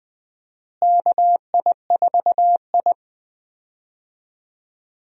Morse code, heard as a keyed tone, about 20 wpm, 700 Hz. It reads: KI4I
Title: USA Call Signs - 4 Characters - Repeated 2x - 20wpm